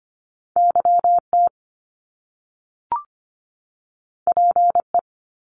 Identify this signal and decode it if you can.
Morse code, heard as a keyed tone, about 25 wpm, 700 Hz. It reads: YT PE
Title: Canadian Provinces and Territories 25wpm